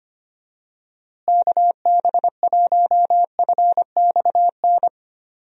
Morse code, heard as a keyed tone, about 25 wpm, 700 Hz. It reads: KB1FXN
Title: USA Call Signs - 6 Characters - Repeated 3x - 25wpm